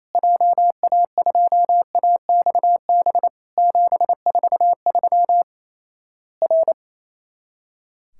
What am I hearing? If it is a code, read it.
Morse code, heard as a keyed tone, about 28 wpm, 700 Hz. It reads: JA2AXB 743 R
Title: Worked All Europe DX Contest - QTC Exchange - 28wpm